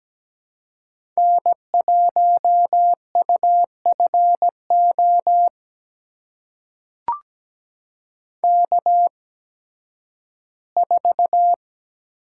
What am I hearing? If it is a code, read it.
Morse code, heard as a keyed tone, about 17 wpm, spaced as for 11 wpm, 700 Hz. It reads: N1UFO K4
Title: USA Call Signs - 5 Characters - 8x Character Spacing - 17wpm